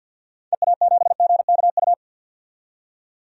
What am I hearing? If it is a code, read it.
Morse code, heard as a keyed tone, about 48 wpm, 700 Hz. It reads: EA7GKU
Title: DXCC-100 Call Signs - 6 Characters - 50wpm